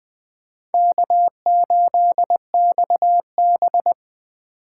Morse code, heard as a keyed tone, about 22 wpm, 700 Hz. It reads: K8XB